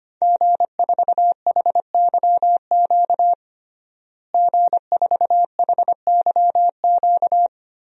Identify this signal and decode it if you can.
Morse code, heard as a keyed tone, about 25 wpm, 700 Hz. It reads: G4HYQ G4HYQ
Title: DXCC-100 Call Signs - 5 Characters - Repeated 2x - 25wpm